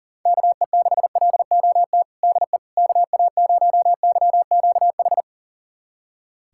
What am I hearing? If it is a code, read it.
Morse code, heard as a keyed tone, about 40 wpm, 700 Hz. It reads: KE6LOT DE KA0YQH
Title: USA Callsign DE Callsign 40wpm